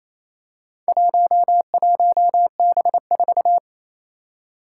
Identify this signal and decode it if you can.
Morse code, heard as a keyed tone, about 28 wpm, 700 Hz. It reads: 11B4
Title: Four Challenging Letter-Number Combinations 28wpm